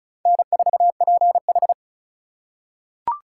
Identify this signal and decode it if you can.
Morse code, heard as a keyed tone, about 34 wpm, 700 Hz. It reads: N4PH
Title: USA Call Signs - 4 Characters - 3x Character Spacing - 35wpm